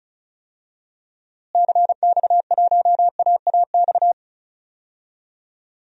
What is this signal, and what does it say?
Morse code, heard as a keyed tone, about 34 wpm, 700 Hz. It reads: CX1AAX
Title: DXCC-100 Call Signs - 6 Characters - Repeated 3x - 35wpm